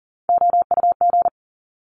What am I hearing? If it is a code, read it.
Morse code, heard as a keyed tone, about 39 wpm, 700 Hz. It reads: OUG
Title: Top Three Letter Combinations in Words 40wpm